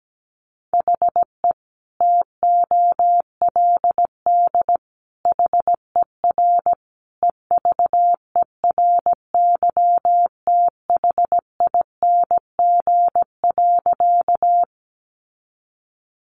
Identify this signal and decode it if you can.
Morse code, heard as a keyed tone, about 17 wpm, 700 Hz. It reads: HE TOLD HER EVERYTHING.
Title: Sentences from Top 700 Words 17wpm